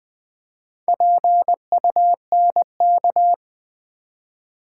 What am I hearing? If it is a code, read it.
Morse code, heard as a keyed tone, about 20 wpm, 690 Hz. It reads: PUNK
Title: Build A Word - 2 Letters At A Time - 4-Letter Words - 20wpm